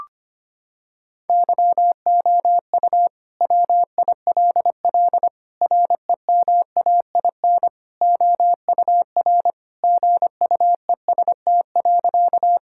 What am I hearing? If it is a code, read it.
Morse code, heard as a keyed tone, about 25 wpm, 700 Hz. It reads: YOU WILL REMAIN OUR GUEST.
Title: Sentences from Top 1600 Words 25wpm